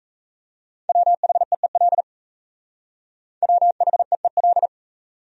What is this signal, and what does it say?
Morse code, heard as a keyed tone, about 35 wpm, 700 Hz. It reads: WHEEL WHEEL
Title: Speed-Racing Top 500 Words 28wpm